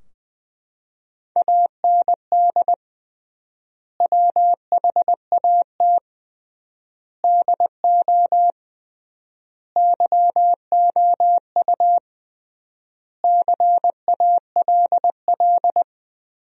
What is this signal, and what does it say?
Morse code, heard as a keyed tone, about 20 wpm, 700 Hz. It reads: AND WHAT DO YOU CALL
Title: 3x Word-Spacing - Sentences from Top 100 Words - 20wpm